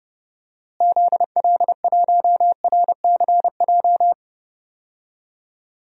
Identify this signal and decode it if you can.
Morse code, heard as a keyed tone, about 32 wpm, 700 Hz. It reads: ZL1RCJ
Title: DXCC-100 Call Signs - 6 Characters - Repeated 2x - 30wpm